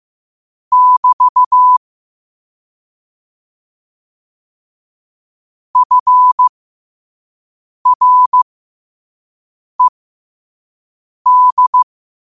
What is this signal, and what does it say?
Morse code, heard as a keyed tone, about 15 wpm, spaced as for 5 wpm, 1000 Hz. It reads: <BT> FRED
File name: qotdfiles / 5wpm